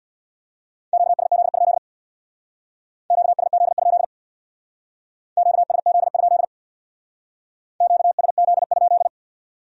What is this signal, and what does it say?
Morse code, heard as a keyed tone, about 63 wpm, 700 Hz. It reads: QSZ? QSZ? QSZ? QSZ?